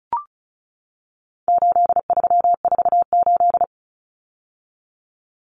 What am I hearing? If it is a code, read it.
Morse code, heard as a keyed tone, about 34 wpm, 700 Hz. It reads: 8348
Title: Common Cut Numbers - 4-Digit 35wpm